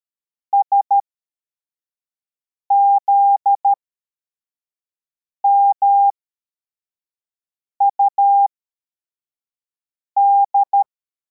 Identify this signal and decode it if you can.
Morse code, heard as a keyed tone, about 13 wpm, spaced as for 4 wpm, 800 Hz. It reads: SZMUD